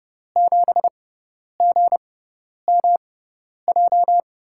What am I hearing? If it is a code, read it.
Morse code, heard as a keyed tone, about 30 wpm, spaced as for 10 wpm, 700 Hz. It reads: 7GMJ
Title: USA Call Signs - 5 Characters - 6x Character Spacing - 30wpm